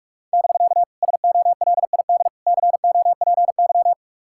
Morse code, heard as a keyed tone, about 55 wpm, 700 Hz. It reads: <BK> SOLID COPY
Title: Speed-Racing Common QSO Phrases Encore - 45wpm